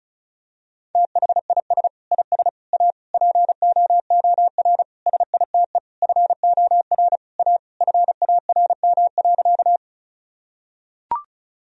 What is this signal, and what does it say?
Morse code, heard as a keyed tone, about 36 wpm, 700 Hz. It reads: THIS IS A POOR SITE FOR A FARM.
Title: Sentences from Top 1000 Words - Review Part 4 - 35wpm